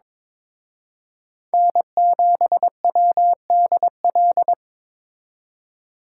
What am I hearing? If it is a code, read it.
Morse code, heard as a keyed tone, about 25 wpm, 700 Hz. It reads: N7WDL
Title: Speed Racing - USA 1x3 Call Signs 22wpm